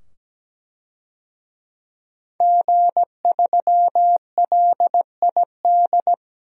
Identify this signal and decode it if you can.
Morse code, heard as a keyed tone, about 17 wpm, 700 Hz. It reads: G3LID
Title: DXCC-100 Call Signs - 5 Characters - 17wpm